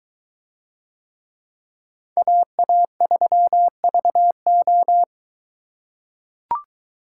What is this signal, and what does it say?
Morse code, heard as a keyed tone, about 21 wpm, 700 Hz. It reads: AA3VO